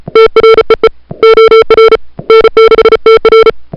Morse code, heard as a keyed tone, about 32 wpm, 440 Hz, with noise. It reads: TREE OR N6TR
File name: n6tr / Fab2007CWSprintHour1